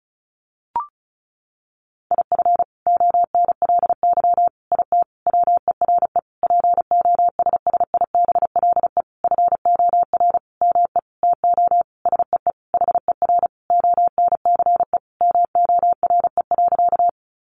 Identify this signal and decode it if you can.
Morse code, heard as a keyed tone, about 34 wpm, 700 Hz. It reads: IF ONLY IT WERE POSSIBLE FOR ME TO SEE HER ONCE MORE.
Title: Sentences from Top 500 Words 35wpm